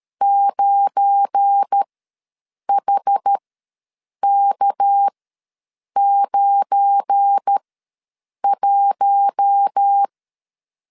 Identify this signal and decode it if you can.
Morse code, heard as a keyed tone, about 13 wpm, spaced as for 7 wpm, 800 Hz. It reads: 9HK91
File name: LEZIONI / 7 GF9Z / 7 GF9 ws 12wpm 3344